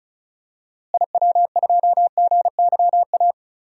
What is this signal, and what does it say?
Morse code, heard as a keyed tone, about 36 wpm, 700 Hz. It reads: IW2GYA
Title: DXCC-100 Call Signs - 6 Characters - Repeated 2x - 35wpm